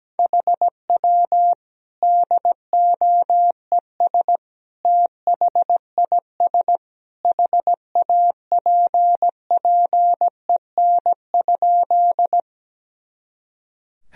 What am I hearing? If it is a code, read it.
Morse code, heard as a keyed tone, about 17 wpm, 700 Hz. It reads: HW DOES THIS HAPPEN?
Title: Sentences from Top 400 Words and Abbreviations 17wpm